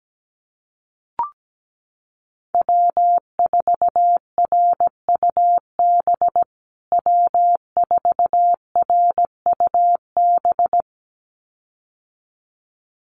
Morse code, heard as a keyed tone, about 17 wpm, 700 Hz. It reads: W4RUB W4RUB
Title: USA Call Signs Repeated 2x 17wpm